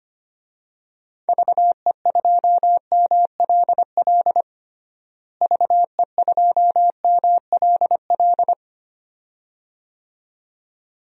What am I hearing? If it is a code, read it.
Morse code, heard as a keyed tone, about 25 wpm, 700 Hz. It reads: VE2MLL VE2MLL
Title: DXCC-100 Call Signs - 6 Characters - Repeated 2x - 25wpm